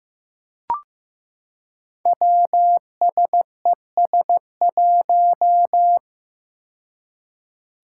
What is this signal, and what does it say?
Morse code, heard as a keyed tone, about 15 wpm, 700 Hz. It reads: WSES1
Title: Five Challenging Letter-Number Combinations 15wpm